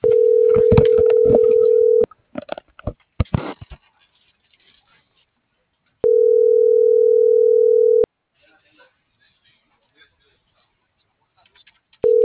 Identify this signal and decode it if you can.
Ringback tone (North America), 440 + 480 Hz.